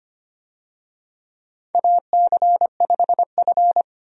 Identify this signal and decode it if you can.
Morse code, heard as a keyed tone, about 25 wpm, 700 Hz. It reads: AC5F
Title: USA Call Signs - 4 Characters - 25wpm